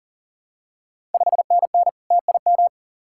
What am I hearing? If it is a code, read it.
Morse code, heard as a keyed tone, about 38 wpm, 700 Hz. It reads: 5NN TIM
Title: Ontario QSO Party Exchange 40wpm